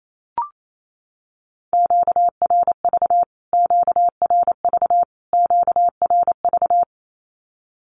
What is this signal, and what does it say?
Morse code, heard as a keyed tone, about 28 wpm, 700 Hz. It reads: QRV QRV QRV
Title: Q-Codes Repeated 3x 28wpm